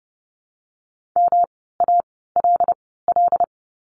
Morse code, heard as a keyed tone, about 30 wpm, spaced as for 17 wpm, 700 Hz. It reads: MALL
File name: course / 30:10wpm / 108 Lesson 18f - Letter m - Words 30wpm